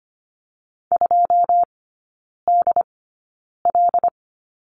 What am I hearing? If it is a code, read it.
Morse code, heard as a keyed tone, about 25 wpm, spaced as for 9 wpm, 700 Hz. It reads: 2DL
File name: course / 25:10wpm / 093 Lesson 16e - Letter c - Three Characters 25wpm